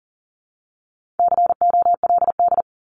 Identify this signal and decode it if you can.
Morse code, heard as a keyed tone, about 40 wpm, 700 Hz. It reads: COLD
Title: Top 500 Words Repeated 40wpm